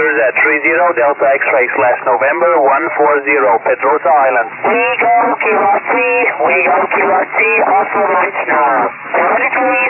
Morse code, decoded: none